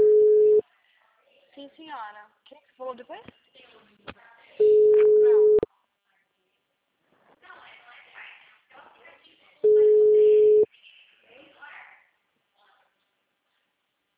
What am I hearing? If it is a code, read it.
Ringback tone (Europe, 425 Hz), with speech.